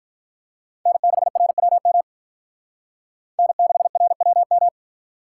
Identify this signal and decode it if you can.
Morse code, heard as a keyed tone, about 44 wpm, 700 Hz. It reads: N6RWM N6RWM